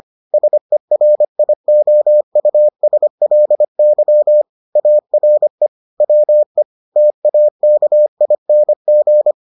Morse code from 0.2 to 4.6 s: SERIOUSLY

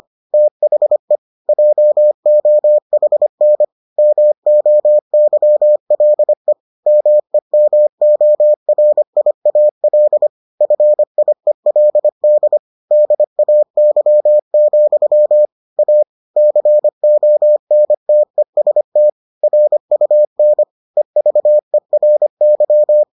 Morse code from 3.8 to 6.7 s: MOYLE

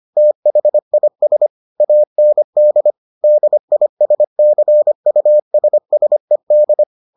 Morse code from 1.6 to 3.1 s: AND

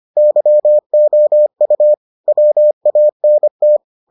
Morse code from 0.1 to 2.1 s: YOU

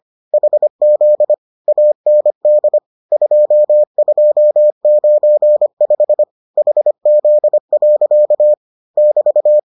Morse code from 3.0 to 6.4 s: 2295

